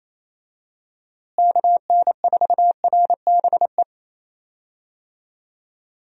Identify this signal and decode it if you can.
Morse code, heard as a keyed tone, about 28 wpm, 700 Hz. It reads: KN4RBE
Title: USA Call Signs - 6 Characters - Repeated 3x - 28wpm